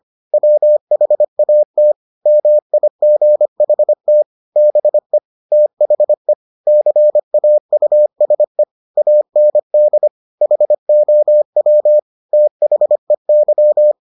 Morse code, heard as a keyed tone, about 25 wpm, 600 Hz. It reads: WHAT MIGHT BE THE CAUSE AND HOW THEY